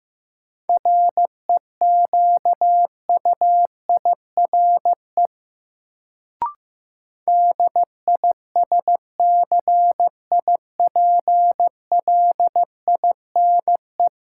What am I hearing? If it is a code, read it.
Morse code, heard as a keyed tone, about 15 wpm, 700 Hz. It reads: REQUIRE DISCIPLINE
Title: Sentences from Top 900 Words - New Word Drill - 15wpm